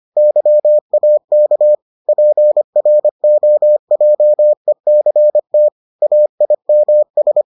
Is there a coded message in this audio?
Yes: Morse code from 0.2 to 7.4 s, heard as a keyed tone, about 25 wpm, 600 Hz. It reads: YAK PROJECT AIMS